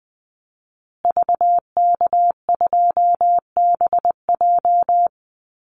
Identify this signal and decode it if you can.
Morse code, heard as a keyed tone, about 19 wpm, 700 Hz. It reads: VK2BJ
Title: Call Signs Repeated 2x - 20wpm